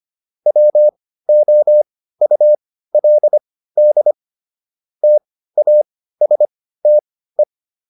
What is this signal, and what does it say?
Morse code, heard as a keyed tone, about 25 wpm, spaced as for 15 wpm, 600 Hz. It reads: WOULD TASTE